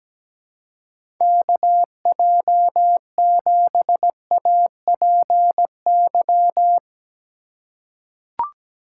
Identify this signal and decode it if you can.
Morse code, heard as a keyed tone, about 17 wpm, 700 Hz. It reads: KJ7APY
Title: USA Call Signs - 6 Characters - 4x Character Spacing - 17wpm